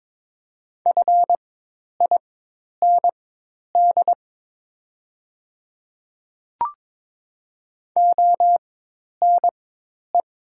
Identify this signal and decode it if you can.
Morse code, heard as a keyed tone, about 22 wpm, spaced as for 10 wpm, 700 Hz. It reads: FIND ONE
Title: 4x Character Spacing - Top 100 Words - 22wpm